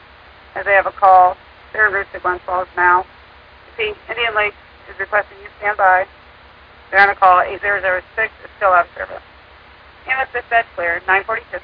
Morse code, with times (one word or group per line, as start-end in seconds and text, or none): none